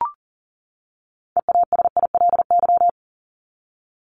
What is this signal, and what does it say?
Morse code, heard as a keyed tone, about 40 wpm, 700 Hz. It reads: EASILY
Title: LY Suffix - Top 100 Words with Base Words 40wpm